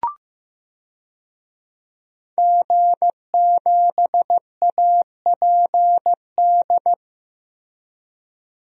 Morse code, heard as a keyed tone, about 15 wpm, 700 Hz. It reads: G7APD
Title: DXCC-100 Call Signs - 5 Characters - Repeated 3x - 15wpm